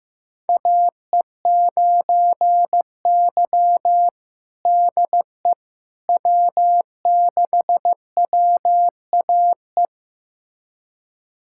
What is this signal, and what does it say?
Morse code, heard as a keyed tone, about 16 wpm, 700 Hz. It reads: AE9Y DE W6WAE
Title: USA Callsign DE Callsign 15wpm